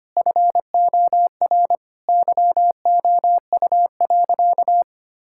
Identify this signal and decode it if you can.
Morse code, heard as a keyed tone, about 25 wpm, 700 Hz. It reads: FOR YOU.
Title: Sentences from Top 1000 Words - Review Part 6 - 25wpm